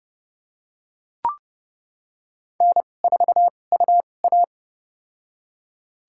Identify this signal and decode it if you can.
Morse code, heard as a keyed tone, about 30 wpm, spaced as for 22 wpm, 700 Hz. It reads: N4UA
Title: USA Call Signs - 4 Characters - 2x Character Spacing - 30wpm